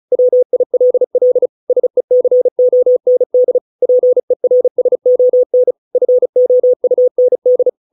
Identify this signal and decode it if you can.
Morse code, heard as a keyed tone, about 37 wpm, 500 Hz. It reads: WILL SECOND PERSON FOUND